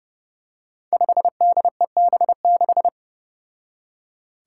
Morse code, heard as a keyed tone, about 31 wpm, 700 Hz. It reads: 5DEB6
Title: Five Challenging Letter-Number Combinations 30wpm